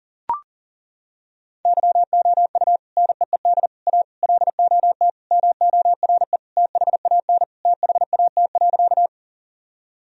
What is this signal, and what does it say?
Morse code, heard as a keyed tone, about 40 wpm, 700 Hz. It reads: YOU NEED A LOT MORE THAN THAT.
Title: Sentences from Top 1000 Words - Review Part 3 - 40wpm